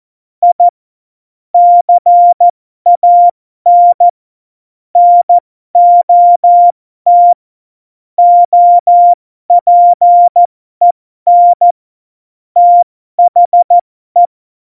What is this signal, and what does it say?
Morse code, heard as a keyed tone, about 14 wpm, spaced as for 12 wpm, 700 Hz. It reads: I CAN NOT OPEN THE
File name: Morsekurs / Witze